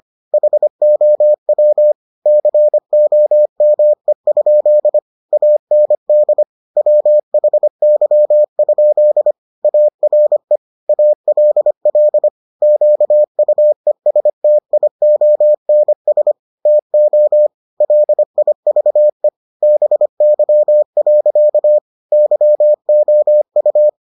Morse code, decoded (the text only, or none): HOW COME? AND WHY? ARE ALL QUESTIONS TO LIVE BY. YOU